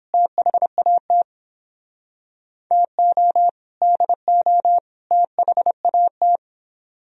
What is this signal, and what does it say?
Morse code, heard as a keyed tone, about 29 wpm, 700 Hz. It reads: THAT TO DO THAT